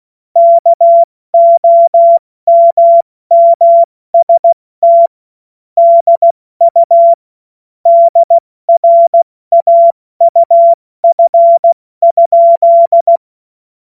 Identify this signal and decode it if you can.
Morse code, heard as a keyed tone, about 16 wpm, 680 Hz. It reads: KOMMST DU DRAUF?
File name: Morsekurs / Witze / Witz 03. Mai 2022 70 F80